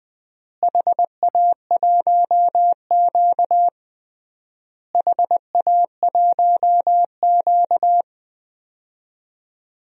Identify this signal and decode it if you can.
Morse code, heard as a keyed tone, about 20 wpm, 700 Hz. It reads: HA1Q HA1Q
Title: DXCC-100 Call Signs - 4 Characters - Repeated 3x - 20wpm